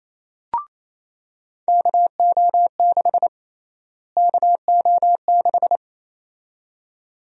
Morse code, heard as a keyed tone, about 28 wpm, 700 Hz. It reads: KO6 KO6
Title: USA Call Sign Prefixes - Repeated 2x - 28wpm